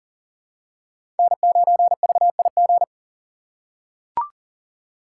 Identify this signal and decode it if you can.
Morse code, heard as a keyed tone, about 40 wpm, 700 Hz. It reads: N9VIG
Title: USA Call Signs - 5 Characters - 6x Character Spacing - 40wpm